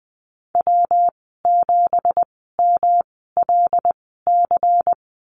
Morse code, heard as a keyed tone, about 20 wpm, spaced as for 15 wpm, 700 Hz. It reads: W7MLC
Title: 207 Lesson 33g - Letter j - Call Signs 20wpm